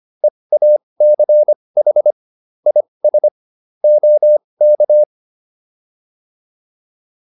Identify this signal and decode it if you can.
Morse code, heard as a keyed tone, about 25 wpm, spaced as for 20 wpm, 600 Hz. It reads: EACH IS OK
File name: PR303 20